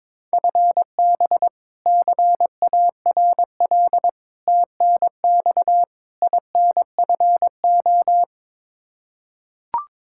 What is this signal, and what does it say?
Morse code, heard as a keyed tone, about 22 wpm, 700 Hz. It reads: FB CARL TNX INFO
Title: Practice QSO - Exchange 3 - More Info - 22wpm